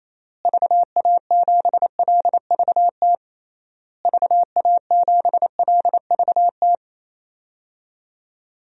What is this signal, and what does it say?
Morse code, heard as a keyed tone, about 28 wpm, 700 Hz. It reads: VA7LVT VA7LVT
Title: DXCC-100 Call Signs - 6 Characters - Repeated 3x - 28wpm